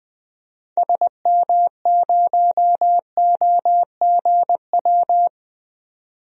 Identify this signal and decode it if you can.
Morse code, heard as a keyed tone, about 20 wpm, 700 Hz. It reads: SM0OGW
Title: DXCC-100 Call Signs - 6 Characters - 20wpm